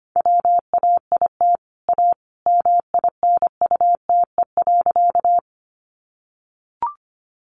Morse code, heard as a keyed tone, about 25 wpm, 700 Hz. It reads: WAIT A MINUTE.
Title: Sentences from Top 500 Words 25wpm